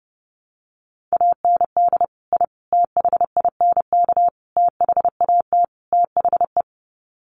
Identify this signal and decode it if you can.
Morse code, heard as a keyed tone, about 30 wpm, 700 Hz. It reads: AND I THINK THAT THE
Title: Sets of 5 Words 30wpm